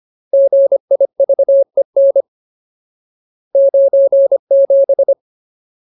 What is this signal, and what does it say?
Morse code, heard as a keyed tone, about 25 wpm, 550 Hz. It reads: GIVEN 97
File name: Ultimate Word List / 25WPM Ultimate